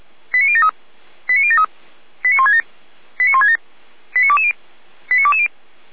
Five-tone selective call (ZVEI), heard as quick runs of five tones, tones of about 70 ms. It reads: 89073 89073 89167 89167 89220 89220